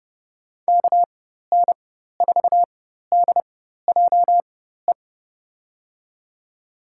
Morse code, heard as a keyed tone, about 30 wpm, spaced as for 14 wpm, 700 Hz. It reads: KN4DJE
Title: USA Call Signs - 6 Characters - 4x Character Spacing - 30wpm